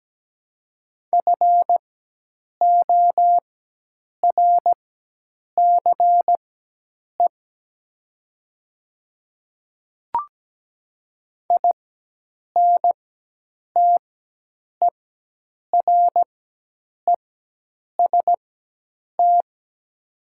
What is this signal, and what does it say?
Morse code, heard as a keyed tone, about 17 wpm, spaced as for 8 wpm, 700 Hz. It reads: FORCE INTEREST
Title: 4x Character Spacing - Top 500 Words - 17wpm